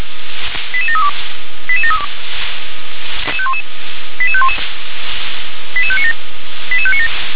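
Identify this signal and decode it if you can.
Five-tone selective call (ZVEI), heard as quick runs of five tones, tones of about 70 ms. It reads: 88732 88732 88511 88597 88597
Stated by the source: ZVEI five-tone selective call received on an SDR.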